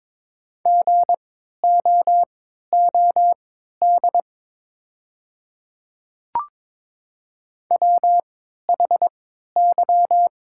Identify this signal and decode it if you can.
Morse code, heard as a keyed tone, about 22 wpm, spaced as for 12 wpm, 700 Hz. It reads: GOOD WHY
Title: 3x Character Spacing - Top 200 Words - 22wpm